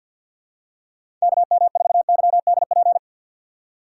Morse code, heard as a keyed tone, about 50 wpm, 700 Hz. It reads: KM4YDP
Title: USA Call Signs - 6 Characters - 3x Character Spacing - 50wpm